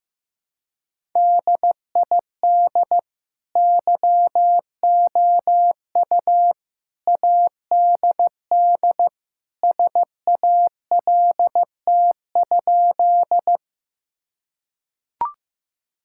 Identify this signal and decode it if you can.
Morse code, heard as a keyed tone, about 15 wpm, 700 Hz. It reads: DID YOU ADD SALT?
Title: Sentences from Top 1000 Words - Review Part 5 - 15wpm